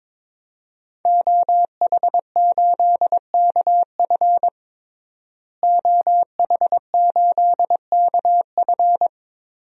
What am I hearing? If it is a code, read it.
Morse code, heard as a keyed tone, about 22 wpm, 700 Hz. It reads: OH8KF OH8KF
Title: DXCC-100 Call Signs - 5 Characters - Repeated 3x - 22wpm